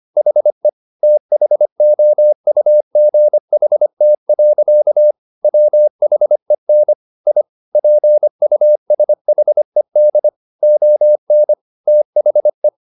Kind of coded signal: Morse code